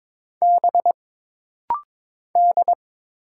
Morse code, heard as a keyed tone, about 22 wpm, 700 Hz. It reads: B D
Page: This rapid-fire timing is geared to push your ICR (Instant Character Recognition) skills.